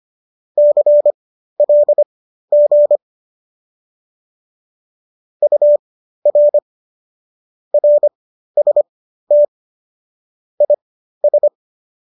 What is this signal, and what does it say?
Morse code, heard as a keyed tone, about 25 wpm, spaced as for 13 wpm, 590 Hz. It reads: CLG UR RST IS